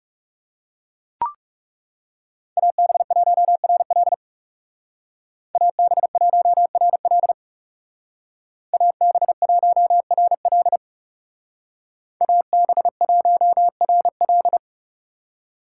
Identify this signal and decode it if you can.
Morse code, heard as a keyed tone, about 37 wpm, 700 Hz. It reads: AB1RL AB1RL AB1RL AB1RL
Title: Speed Racing - USA Call Signs 30wpm